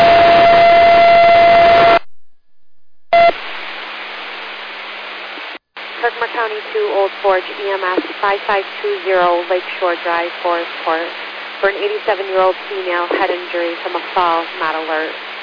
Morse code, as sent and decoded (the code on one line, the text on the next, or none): none